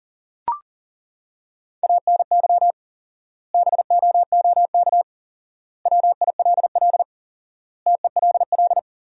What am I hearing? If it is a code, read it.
Morse code, heard as a keyed tone, about 40 wpm, 700 Hz. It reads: ANY BOOK WILL TELL
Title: 4x Word-Spacing - Sentences from Top 300 Words - 40wpm